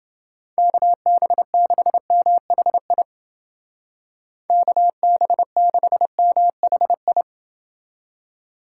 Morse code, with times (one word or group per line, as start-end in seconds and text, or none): 0.6-3.0 KB6MHI
4.5-7.2 KB6MHI